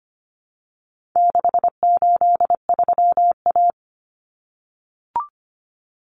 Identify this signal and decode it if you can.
Morse code, heard as a keyed tone, about 25 wpm, 700 Hz. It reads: 683A
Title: Common Cut Numbers - 4-Digit 25wpm